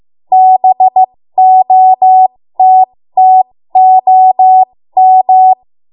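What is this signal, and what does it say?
Morse code, heard as a keyed tone, about 15 wpm, spaced as for 13 wpm, 750 Hz. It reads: BOTTOM
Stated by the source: 13 wpm Morse code practice archive transmission.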